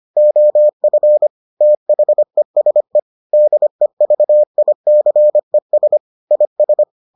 Morse code, heard as a keyed tone, about 25 wpm, 600 Hz. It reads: OF THESE DEVICES IS